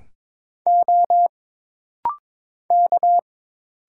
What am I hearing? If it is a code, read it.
Morse code, heard as a keyed tone, about 22 wpm, 700 Hz. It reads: O K
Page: This rapid-fire timing is geared to push your ICR (Instant Character Recognition) skills.